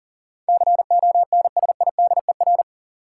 Morse code, heard as a keyed tone, about 39 wpm, 700 Hz. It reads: CONSIDER
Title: Sentences from Top 600 Words - New Word Drill - 40wpm